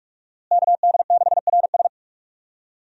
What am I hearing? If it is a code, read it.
Morse code, heard as a keyed tone, about 46 wpm, 700 Hz. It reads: KD6RS
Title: USA Call Signs - 5 Characters - Repeated 3x - 45wpm